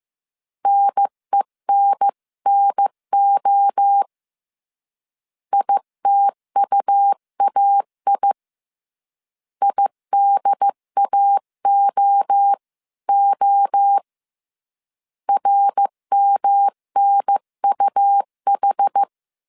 Morse code, heard as a keyed tone, about 15 wpm, 800 Hz. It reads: NENNO ITUAI IDAOO RMNUH
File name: CW / DET / 1 3 1500C S C 14wpm 2458